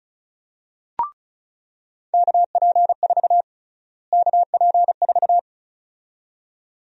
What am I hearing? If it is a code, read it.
Morse code, heard as a keyed tone, about 36 wpm, 700 Hz. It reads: KP4 KP4
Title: USA Call Sign Prefixes - Repeated 2x - 35wpm